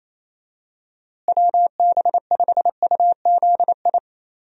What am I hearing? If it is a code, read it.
Morse code, heard as a keyed tone, about 27 wpm, 700 Hz. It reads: WB5UZI
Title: USA Call Signs - 6 Characters - Repeated 3x - 28wpm